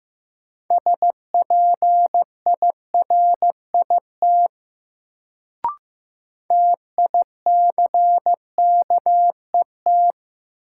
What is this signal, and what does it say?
Morse code, heard as a keyed tone, about 15 wpm, 700 Hz. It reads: SPIRIT TICKET
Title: Sentences from Top 1400 Words - New Word Drill - 15wpm